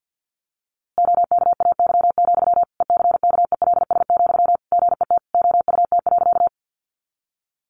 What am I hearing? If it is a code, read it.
Morse code, heard as a keyed tone, about 50 wpm, 700 Hz. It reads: OKAY, ECKELS, GET OUT.
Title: A Sound Of Thunder 50wpm